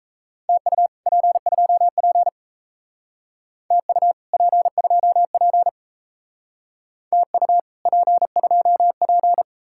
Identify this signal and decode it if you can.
Morse code, heard as a keyed tone, about 33 wpm, 700 Hz. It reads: TU P2P TU P2P TU P2P
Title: Speed-Racing Common QSO Phrases Encore - 28wpm